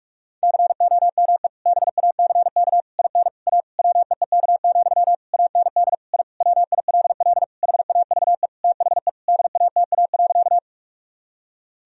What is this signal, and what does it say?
Morse code, heard as a keyed tone, about 46 wpm, 700 Hz. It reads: COME BACK IN A WEEK, AND I WILL HAVE THE DATA.
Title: Sentences from Top 1000 Words - Review Part 3 - 45wpm